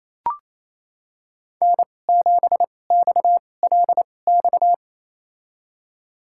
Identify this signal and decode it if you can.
Morse code, heard as a keyed tone, about 28 wpm, spaced as for 20 wpm, 700 Hz. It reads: N7XLX
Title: USA Call Signs - 5 Characters - 2x Character Spacing - 28wpm